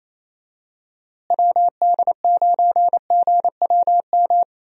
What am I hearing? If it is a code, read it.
Morse code, heard as a keyed tone, about 28 wpm, 700 Hz. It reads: WD9GWM
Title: USA Call Signs - 6 Characters - Repeated 2x - 28wpm